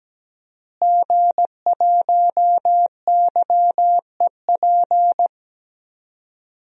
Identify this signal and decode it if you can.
Morse code, heard as a keyed tone, about 17 wpm, 700 Hz. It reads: G1YEP
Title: DXCC-100 Call Signs - 5 Characters - Repeated 2x - 17wpm